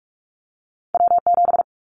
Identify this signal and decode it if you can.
Morse code, heard as a keyed tone, about 44 wpm, 700 Hz. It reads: W7
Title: Callsign Prefixes for Top 25 DXCC - Repeated 3x - 45wpm